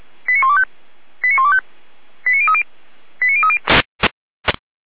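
Five-tone selective call (ZVEI), heard as quick runs of five tones, tones of about 70 ms. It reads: 89125 89125 89030 89030
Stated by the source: ZVEI five-tone call received by SDR.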